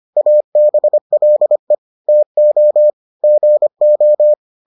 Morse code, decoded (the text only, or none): ABLE TO GO